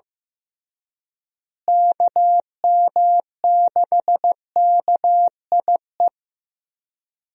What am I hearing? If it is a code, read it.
Morse code, heard as a keyed tone, about 14 wpm, 700 Hz. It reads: KM6KIE
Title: USA Call Signs - 6 Characters - Repeated 3x - 15wpm